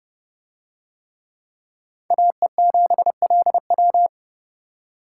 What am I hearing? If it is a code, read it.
Morse code, heard as a keyed tone, about 29 wpm, 700 Hz. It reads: AE7LW